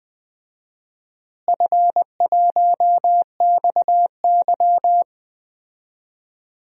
Morse code, heard as a keyed tone, about 20 wpm, 700 Hz. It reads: F1XY
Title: DXCC-100 Call Signs - 4 Characters - Repeated 2x - 20wpm